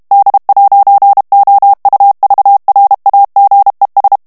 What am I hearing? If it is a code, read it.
Morse code, heard as a keyed tone, about 32 wpm, 800 Hz. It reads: D'OUVRAGES
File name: FRANCAIS TEXTE CLAIR 2 VITESSE 30 MOTS MINUTES